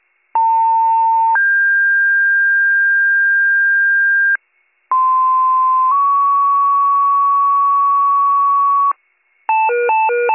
Two-tone page, heard as a one-second tone, then a three-second tone, about 900 then 1600 Hz, about 1040 then 1120 Hz. It is heard from 0.3 to 8.9 s.